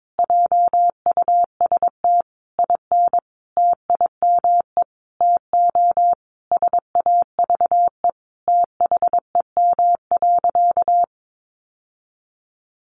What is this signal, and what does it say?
Morse code, heard as a keyed tone, about 22 wpm, 700 Hz. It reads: JUST IN TIME TO SAVE THEM.
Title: Sentences from Top 1100 Words 22wpm